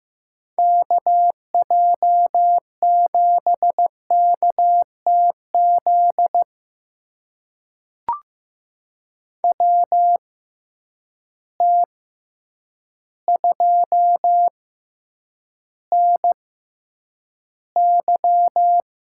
Morse code, heard as a keyed tone, about 15 wpm, spaced as for 8 wpm, 700 Hz. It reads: KJ7KTZ WT2NY
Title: USA Call Signs - 6x Character Spacing - 15wpm